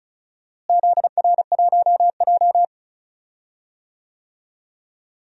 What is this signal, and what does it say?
Morse code, heard as a keyed tone, about 36 wpm, 700 Hz. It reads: ZR1J
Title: DXCC-100 Call Signs - 4 Characters - Repeated 2x - 35wpm